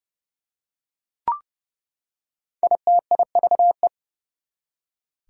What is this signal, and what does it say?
Morse code, heard as a keyed tone, about 29 wpm, 700 Hz. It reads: ITIVE
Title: Prefixes and Suffixes 30wpm